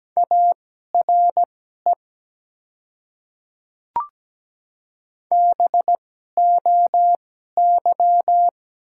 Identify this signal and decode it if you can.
Morse code, heard as a keyed tone, about 17 wpm, spaced as for 12 wpm, 700 Hz. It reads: ARE BOY
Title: 2x Character Spacing - Top 300 Words - 17wpm